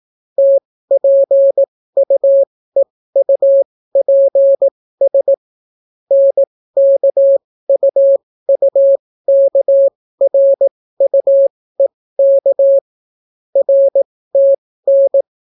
Morse code, heard as a keyed tone, about 18 wpm, spaced as for 14 wpm, 550 Hz. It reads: TPUEUPS NKUUKRUEK RTN